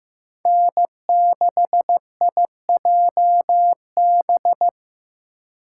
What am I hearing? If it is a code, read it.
Morse code, heard as a keyed tone, about 18 wpm, 700 Hz. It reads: N6IJB
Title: Speed Racing - USA 1x3 Call Signs 15wpm